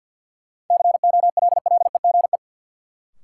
Morse code, heard as a keyed tone, about 52 wpm, 700 Hz. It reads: COLLEGE
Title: Sentences from Top 1300 Words - New Word Drill - 50wpm